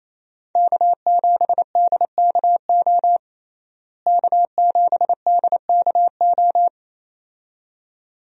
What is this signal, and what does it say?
Morse code, heard as a keyed tone, about 28 wpm, 700 Hz. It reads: K7DKO K7DKO
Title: USA Call Signs - 5 Characters - Repeated 2x - 28wpm